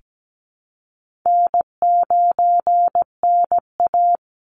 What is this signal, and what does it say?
Morse code, heard as a keyed tone, about 17 wpm, 700 Hz. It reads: N9NA
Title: Easy Call Signs - 17wpm